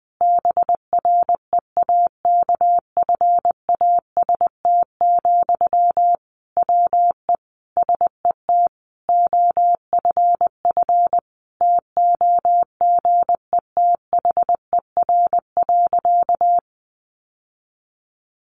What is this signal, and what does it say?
Morse code, heard as a keyed tone, about 20 wpm, 700 Hz. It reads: BREAKFAST, WE SET OFF TOGETHER.
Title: Sentences from Top 1500 Words 20wpm